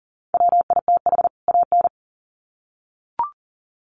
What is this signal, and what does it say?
Morse code, heard as a keyed tone, about 40 wpm, 700 Hz. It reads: WITH AN
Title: Sets of 2 Words 40wpm